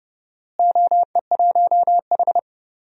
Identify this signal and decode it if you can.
Morse code, heard as a keyed tone, about 30 wpm, 700 Hz. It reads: OE1H
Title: DXCC-100 Call Signs - 4 Characters - Repeated 3x - 30wpm